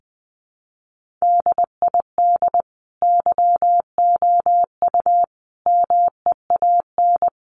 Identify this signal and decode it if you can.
Morse code, heard as a keyed tone, about 20 wpm, 700 Hz. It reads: DID YOU MEAN
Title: Sentences from Top 1300 Words 20wpm